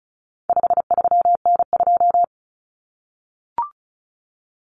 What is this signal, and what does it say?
Morse code, heard as a keyed tone, about 34 wpm, 700 Hz. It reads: 53N2
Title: Common Cut Numbers - 4-Digit 35wpm